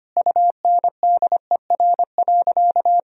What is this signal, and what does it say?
Morse code, heard as a keyed tone, about 25 wpm, 700 Hz. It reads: UNDER.
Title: Sentences from Top 1000 Words - Review Part 1 - 25wpm